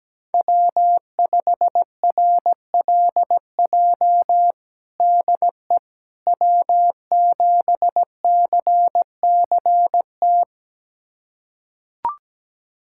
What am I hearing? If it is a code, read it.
Morse code, heard as a keyed tone, about 17 wpm, 700 Hz. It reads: W5RLJ DE W7CCT
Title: USA Callsign DE Callsign - 5 Characters - 17wpm